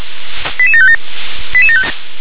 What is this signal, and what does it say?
Five-tone selective call (ZVEI), heard as quick runs of five tones, tones of about 70 ms. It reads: 88757